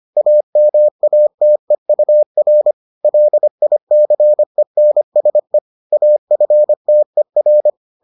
Morse code, heard as a keyed tone, about 25 wpm, 600 Hz. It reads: AMATEUR LICENSE AFTER